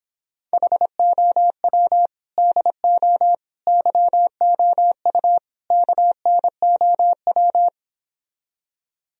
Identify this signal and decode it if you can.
Morse code, heard as a keyed tone, about 29 wpm, 700 Hz. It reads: HOW DO YOU KNOW